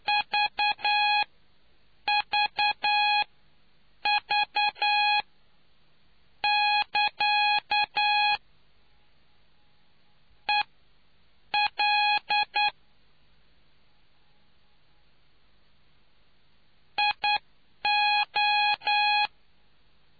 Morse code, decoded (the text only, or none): VVV <KA> EL IO